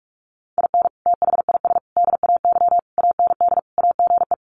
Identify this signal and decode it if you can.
Morse code, heard as a keyed tone, about 44 wpm, 700 Hz. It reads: IN THIS DAY AND AGE
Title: Sets of 5 Words Encore 45wpm